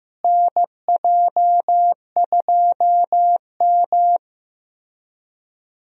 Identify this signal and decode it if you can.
Morse code, heard as a keyed tone, about 15 wpm, 700 Hz. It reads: NJ2M
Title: USA Call Signs - 4 Characters - Repeated 2x - 15wpm